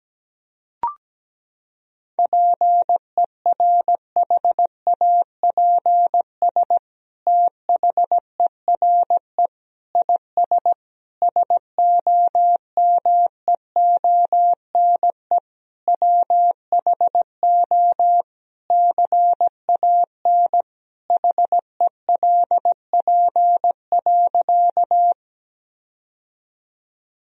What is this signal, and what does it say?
Morse code, heard as a keyed tone, about 17 wpm, 700 Hz. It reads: PERHAPS THERE IS SOMEONE WHO CAN HELP.
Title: Sentences from Top 1000 Words - Review Part 4 - 17wpm